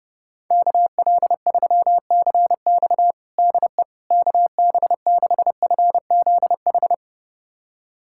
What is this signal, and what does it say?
Morse code, heard as a keyed tone, about 31 wpm, 700 Hz. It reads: KL3CX DE KB6FZH
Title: USA Callsign DE Callsign 30wpm